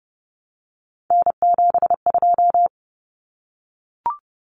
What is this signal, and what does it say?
Morse code, heard as a keyed tone, about 30 wpm, 700 Hz. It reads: N72
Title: Common Cut Numbers - 3-Digit 30wpm